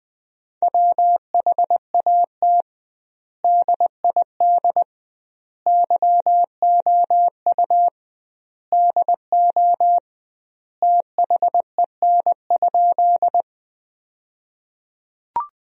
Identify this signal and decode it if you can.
Morse code, heard as a keyed tone, about 20 wpm, 700 Hz. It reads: WHAT DID YOU DO THEN?
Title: 2x Word-Spacing - Sentences from Top 100 Words - 20wpm